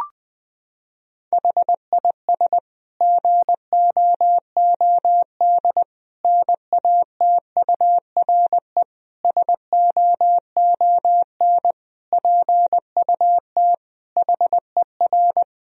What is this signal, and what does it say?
Morse code, heard as a keyed tone, about 20 wpm, 700 Hz. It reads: HIS GOOD NATURE SOON PUT HER
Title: Sentences from Top 1000 Words - Review Part 1 - 20wpm